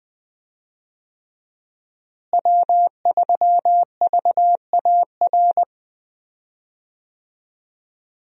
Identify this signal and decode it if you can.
Morse code, heard as a keyed tone, about 21 wpm, 700 Hz. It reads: W3VAR
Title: USA Call Signs - 5 Characters - Repeated 3x - 20wpm